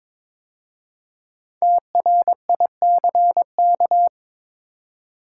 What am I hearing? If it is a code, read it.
Morse code, heard as a keyed tone, about 22 wpm, 700 Hz. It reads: TRICK
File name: 5 Letter Words - First 3 Letters Given - 22wpm